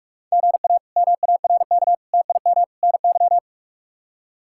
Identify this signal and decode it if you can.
Morse code, heard as a keyed tone, about 46 wpm, 700 Hz. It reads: GA MARK TIM NY
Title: K1USN SST QSO Practice - 45wpm